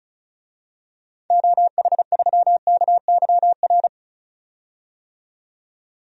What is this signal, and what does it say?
Morse code, heard as a keyed tone, about 36 wpm, 700 Hz. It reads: OH3KYR
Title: DXCC-100 Call Signs - 6 Characters - 35wpm